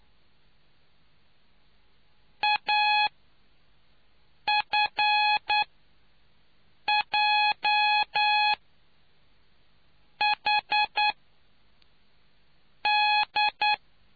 Morse code, heard as a keyed tone, about 9 wpm, spaced as for 4 wpm, 840 Hz. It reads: AFJHD